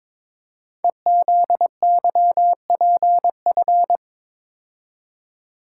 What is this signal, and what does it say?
Morse code, heard as a keyed tone, about 23 wpm, 700 Hz. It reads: EZYPF
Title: Code Groups - 5 Letters - 22wpm